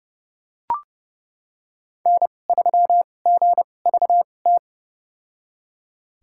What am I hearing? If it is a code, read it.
Morse code, heard as a keyed tone, about 32 wpm, spaced as for 22 wpm, 700 Hz. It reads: N3GVT
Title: USA Call Signs - 5 Characters - 2x Character Spacing - 30wpm